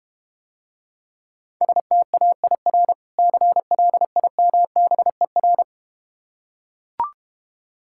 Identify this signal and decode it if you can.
Morse code, heard as a keyed tone, about 32 wpm, 700 Hz. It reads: STAIR CLIMBER